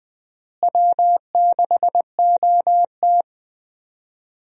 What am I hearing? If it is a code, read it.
Morse code, heard as a keyed tone, about 20 wpm, 700 Hz. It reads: W6OT
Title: USA Call Signs - 4 Characters - Repeated 3x - 20wpm